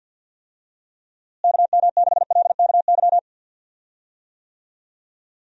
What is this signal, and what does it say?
Morse code, heard as a keyed tone, about 51 wpm, 700 Hz. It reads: KM6LKY
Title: USA Call Signs - 6 Characters - Repeated 2x - 50wpm